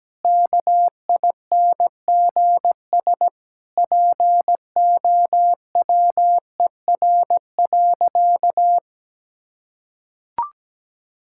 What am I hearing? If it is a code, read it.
Morse code, heard as a keyed tone, about 17 wpm, 700 Hz. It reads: KINGS POWER.
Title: Sentences from Top 1000 Words - Review Part 6 - 17wpm